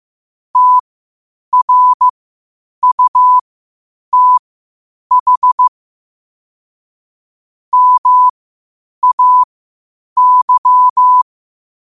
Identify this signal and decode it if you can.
Morse code, heard as a keyed tone, about 15 wpm, spaced as for 8 wpm, 1000 Hz. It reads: TRUTH MAY
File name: qotdfiles / 8wpm